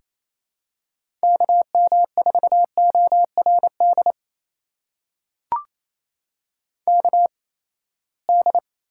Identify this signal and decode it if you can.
Morse code, heard as a keyed tone, about 28 wpm, spaced as for 19 wpm, 700 Hz. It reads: KM4ORD KD
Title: USA Call Signs - 6 Characters - 8x Character Spacing - 28wpm